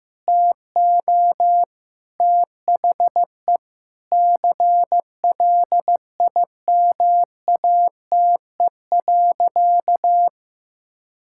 Morse code, heard as a keyed tone, about 15 wpm, 700 Hz. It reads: TO THE CLIMATE.
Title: Sentences from Top 1600 Words 15wpm